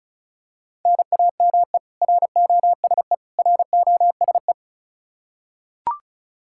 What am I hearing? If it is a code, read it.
Morse code, heard as a keyed tone, about 36 wpm, 700 Hz. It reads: NAME ROSE ROSE
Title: Practice QSO - Exchange 1 - Beginning - 35wpm